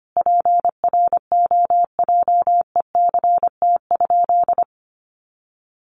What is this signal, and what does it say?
Morse code, heard as a keyed tone, about 25 wpm, 700 Hz. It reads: PROJECT?
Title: Sentences from Top 1400 Words 25wpm